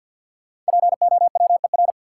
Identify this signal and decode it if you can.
Morse code, heard as a keyed tone, about 53 wpm, 700 Hz. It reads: POWER
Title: Sets of 4 Words - New Word Drill - 50wpm